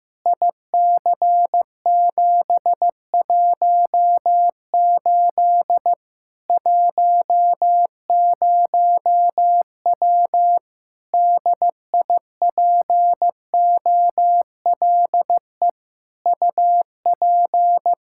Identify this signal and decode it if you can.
Morse code, heard as a keyed tone, about 15 wpm, 700 Hz. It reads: IC718 10W DIPOLE UP
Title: Practice QSO - Exchange 2 - Info - 15wpm